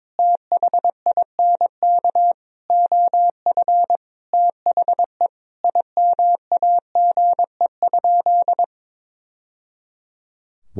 Morse code, heard as a keyed tone, about 22 wpm, 700 Hz. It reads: THINK OF THE IMAGE?
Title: Sentences from Top 1000 Words - Review Part 4 - 22wpm